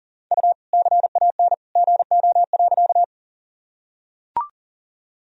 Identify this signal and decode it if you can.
Morse code, heard as a keyed tone, about 40 wpm, 700 Hz. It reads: U CAN GO.
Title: Sentences from Top 100 Words and Abbreviations 40wpm